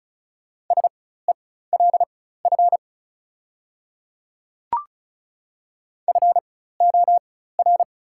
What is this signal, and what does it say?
Morse code, heard as a keyed tone, about 36 wpm, spaced as for 16 wpm, 690 Hz. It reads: SELF FOR
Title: 4x Character Spacing - Top 200 Words - 35wpm